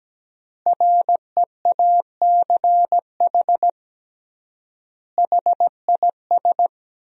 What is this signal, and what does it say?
Morse code, heard as a keyed tone, about 17 wpm, 700 Hz. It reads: REACH HIS
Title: 3x Word-Spacing - Sentences from Top 400 Words - 17wpm